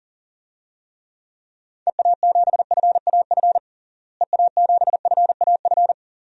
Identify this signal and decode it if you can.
Morse code, heard as a keyed tone, about 37 wpm, 700 Hz. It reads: EA7FAF EA7FAF
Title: DXCC-100 Call Signs - 6 Characters - Repeated 3x - 40wpm